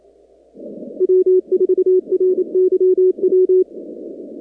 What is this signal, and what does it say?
Morse code, heard as a keyed tone, about 28 wpm, 370 Hz. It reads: W4RYW
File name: contests / 2019-cw-wpx-kt5j-k5pi- -2330 0-45